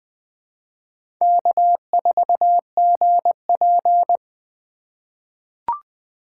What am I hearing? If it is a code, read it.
Morse code, heard as a keyed tone, about 20 wpm, 700 Hz. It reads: K4GP
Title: USA Call Signs - 4 Characters - 2x Character Spacing - 20wpm